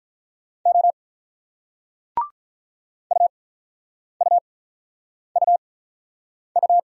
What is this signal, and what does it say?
Morse code, heard as a keyed tone, about 46 wpm, spaced as for 9 wpm, 700 Hz. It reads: O UUUU